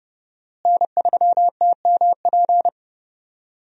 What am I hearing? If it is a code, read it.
Morse code, heard as a keyed tone, about 32 wpm, 700 Hz. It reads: N3TMP
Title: USA Call Signs - 5 Characters - Repeated 2x - 30wpm